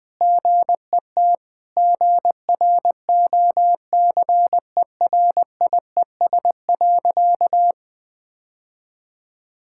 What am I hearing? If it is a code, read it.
Morse code, heard as a keyed tone, about 21 wpm, 700 Hz. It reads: GET GROCERIES.
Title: Sentences with 3-Word Phrases - 20wpm